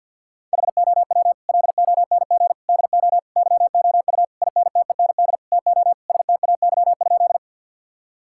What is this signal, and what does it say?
Morse code, heard as a keyed tone, about 52 wpm, 700 Hz. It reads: HOW LONG DO YOU INTEND TO STAY?
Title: Sentences from Top 1600 Words 50wpm